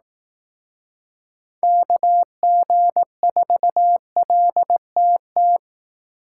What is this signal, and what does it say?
Morse code, heard as a keyed tone, about 18 wpm, 700 Hz. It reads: KG4LTT
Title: Speed Racing - USA 2x3 Call Signs 15wpm